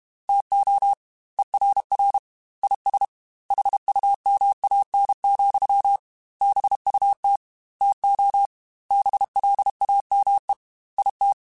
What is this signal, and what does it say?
Morse code, heard as a keyed tone, about 31 wpm, 800 Hz. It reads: TO ERR IS HUMAN, BUT TO BLAME IT